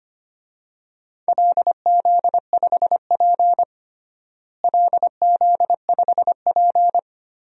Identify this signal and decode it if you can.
Morse code, heard as a keyed tone, about 25 wpm, 700 Hz. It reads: LZ5P LZ5P
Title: DXCC-100 Call Signs - 4 Characters - Repeated 2x - 25wpm